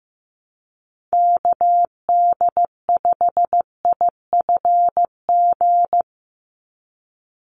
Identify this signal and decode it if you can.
Morse code, heard as a keyed tone, about 15 wpm, 700 Hz. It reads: KD5IFG
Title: USA Call Signs 15wpm